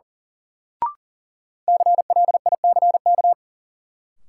Morse code, heard as a keyed tone, about 37 wpm, 700 Hz. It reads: CLICK
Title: Sentences from Top 1100 Words - New Word Drill Alt - 40wpm